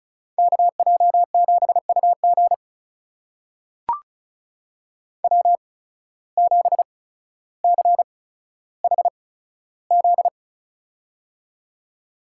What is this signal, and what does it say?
Morse code, heard as a keyed tone, about 36 wpm, spaced as for 15 wpm, 700 Hz. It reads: KJ7UG W7CHZ
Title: USA Call Signs - 5 Characters - 8x Character Spacing - 35wpm